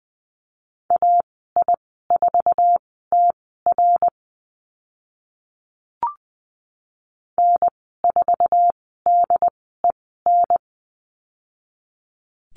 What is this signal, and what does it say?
Morse code, heard as a keyed tone, about 20 wpm, spaced as for 14 wpm, 700 Hz. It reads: AI4TR N4DEN
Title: 069 Lesson 12g - Letter d - Call Signs 20wpm